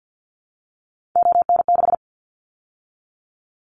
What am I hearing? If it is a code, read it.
Morse code, heard as a keyed tone, about 50 wpm, 700 Hz. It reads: ON6
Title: Callsign Prefixes for Top 100 DXCC - 50wpm